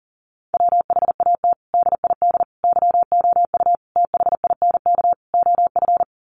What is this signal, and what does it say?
Morse code, heard as a keyed tone, about 40 wpm, 700 Hz. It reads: WHAT DID YOU THINK OF
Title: Sets of 5 Words Encore2 40wpm